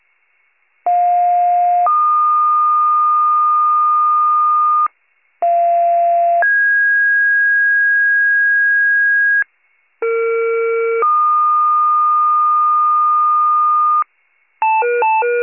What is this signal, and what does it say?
Two-tone page, heard as a one-second tone, then a three-second tone, about 710 then 1220 Hz, about 690 then 1650 Hz, about 460 then 1220 Hz. It is heard from 0.9 to 14.0 s.